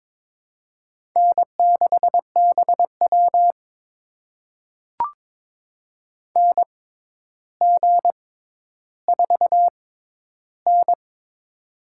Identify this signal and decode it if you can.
Morse code, heard as a keyed tone, about 22 wpm, spaced as for 11 wpm, 700 Hz. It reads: N6BW NG4N
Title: USA Call Signs - 4 Characters - 6x Character Spacing - 22wpm